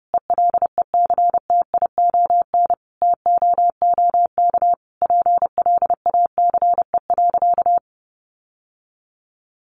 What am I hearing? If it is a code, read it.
Morse code, heard as a keyed tone, about 30 wpm, 700 Hz. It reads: ELECTION TOOK PLACE.
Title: Sentences from Top 1500 Words 30wpm